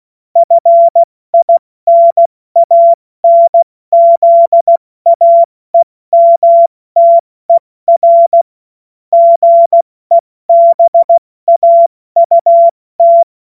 Morse code, heard as a keyed tone, about 16 wpm, spaced as for 14 wpm, 680 Hz. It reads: FINANZAEMTER GEBAUT
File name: Witz-28-Oktober-2025-70-F80